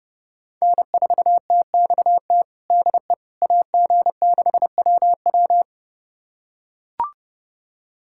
Morse code, heard as a keyed tone, about 30 wpm, 700 Hz. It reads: N4TXT DE AG6WW
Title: USA Callsign DE Callsign - 5 Characters - 30wpm